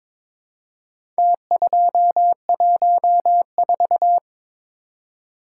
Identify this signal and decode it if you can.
Morse code, heard as a keyed tone, about 22 wpm, 700 Hz. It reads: T214
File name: practice / 22wpm / Four Letter-Number Combinations - 22wpm